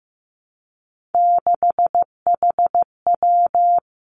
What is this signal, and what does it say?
Morse code, heard as a keyed tone, about 15 wpm, 700 Hz. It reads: 6HW
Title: Three Challenging Letter-Number Combinations 15wpm